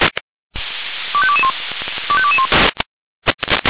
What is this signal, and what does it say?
Five-tone selective call (ZVEI), heard as quick runs of five tones, tones of about 70 ms. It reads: 25221 25221